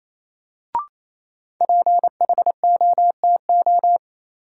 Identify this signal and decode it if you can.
Morse code, heard as a keyed tone, about 28 wpm, 700 Hz. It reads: PHOTO
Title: Sentences from Top 1200 Words - New Word Drill Alt - 28wpm